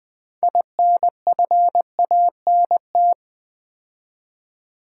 Morse code, heard as a keyed tone, about 20 wpm, 700 Hz. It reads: INFANT
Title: Build A Word - 2 Letters At A Time - 6-Letter Words - 20wpm